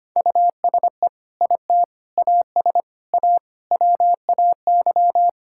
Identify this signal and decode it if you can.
Morse code, heard as a keyed tone, about 25 wpm, 700 Hz. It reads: USE IT AS A WAY
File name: practice / 25wpm / Sentences with 4-Word Phrases - 25wpm